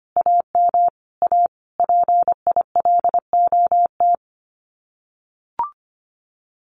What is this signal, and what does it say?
Morse code, heard as a keyed tone, about 25 wpm, 700 Hz. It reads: AM A PILOT
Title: Common QSO Phrases Encore 25wpm